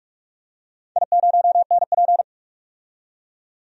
Morse code, heard as a keyed tone, about 42 wpm, 700 Hz. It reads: I0NP
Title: DXCC-100 Call Signs - 4 Characters - 45wpm